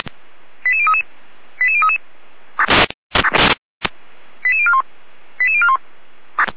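Five-tone selective call (ZVEI), heard as quick runs of five tones, tones of about 70 ms. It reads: 88033 88033 88041 88041
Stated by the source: ZVEI five-tone selective call received on an SDR.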